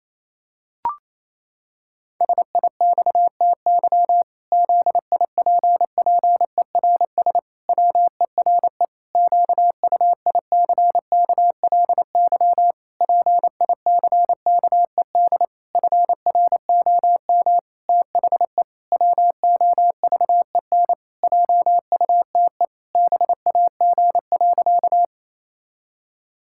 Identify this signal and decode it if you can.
Morse code, heard as a keyed tone, about 28 wpm, 700 Hz. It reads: SIXTY ZIPPERS WERE QUICKLY PICKED FROM THE WOVEN JUTE BAG.
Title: Pangram Challenge v2 - 28wpm